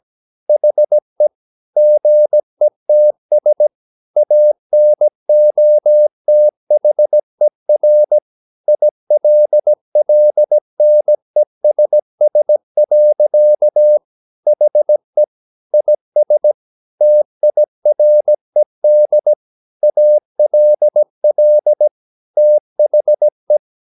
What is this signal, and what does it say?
Morse code, heard as a keyed tone, about 17 wpm, 600 Hz. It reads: HE GETS ANOTHER ILLNESS. HE IS TIRED ALL THE
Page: Romanen Never af Ken Follet på engelsk.